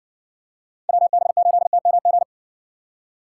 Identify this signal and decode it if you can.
Morse code, heard as a keyed tone, about 58 wpm, 700 Hz. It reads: WB8TMG